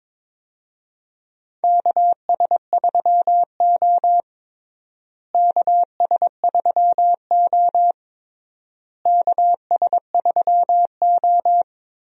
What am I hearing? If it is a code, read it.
Morse code, heard as a keyed tone, about 22 wpm, 700 Hz. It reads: KS3O KS3O KS3O
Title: USA Call Signs - 4 Characters - Repeated 3x - 22wpm